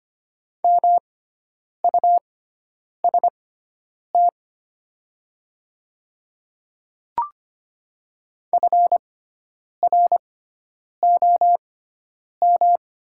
Morse code, heard as a keyed tone, about 25 wpm, spaced as for 9 wpm, 700 Hz. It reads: MUST FROM
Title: 6x Character Spacing - Top 200 Words - 25wpm